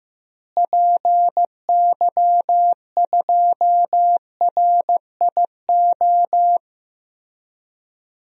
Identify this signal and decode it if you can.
Morse code, heard as a keyed tone, about 16 wpm, 700 Hz. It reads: PY2RIO
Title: DXCC-100 Call Signs - 6 Characters - Repeated 3x - 15wpm